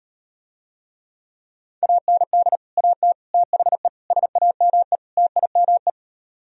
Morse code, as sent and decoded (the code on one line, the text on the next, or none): .- -. -.. / .- - / - .... / ... .- -- . / - .. -- .
AND AT THE SAME TIME